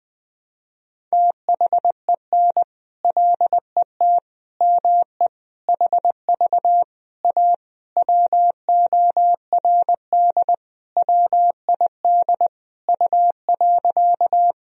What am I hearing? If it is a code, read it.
Morse code, heard as a keyed tone, about 20 wpm, 700 Hz. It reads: THEN LET ME HV A WORD WID U.
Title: Sentences from Top 300 Words and Abbreviations 20wpm